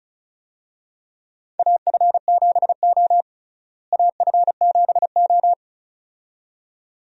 Morse code, heard as a keyed tone, about 36 wpm, 700 Hz. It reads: AF7O AF7O
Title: USA Call Signs - 4 Characters - Repeated 2x - 35wpm